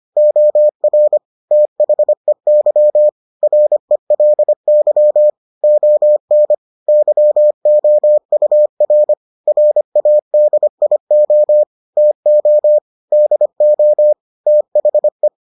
Morse code, heard as a keyed tone, about 25 wpm, 600 Hz. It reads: OR THEY RELY ON YOUR RADIO TO DO THE